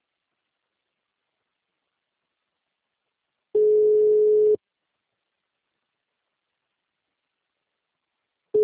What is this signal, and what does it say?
Ringback tone (Europe, 425 Hz).
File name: out- -2007- - - .7536